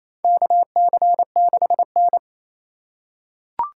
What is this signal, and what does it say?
Morse code, heard as a keyed tone, about 28 wpm, 700 Hz. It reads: KC6N
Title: USA Call Signs - 4 Characters - 8x Character Spacing - 28wpm